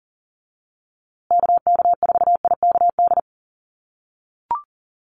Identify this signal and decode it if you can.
Morse code, heard as a keyed tone, about 40 wpm, 700 Hz. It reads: KK4IKD
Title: USA Call Signs Repeated 2x 40wpm